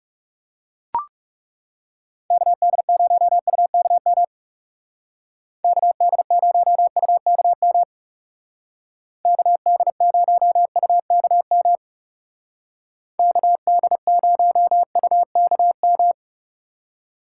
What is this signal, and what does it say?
Morse code, heard as a keyed tone, about 36 wpm, 700 Hz. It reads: KD0UKM KD0UKM KD0UKM KD0UKM
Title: Speed Racing - USA 2x3 Call Signs 30wpm